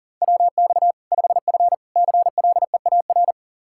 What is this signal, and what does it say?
Morse code, heard as a keyed tone, about 37 wpm, 700 Hz. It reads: WX 5F CLEAR
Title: Practice QSO - Exchange 2 - Info - 40wpm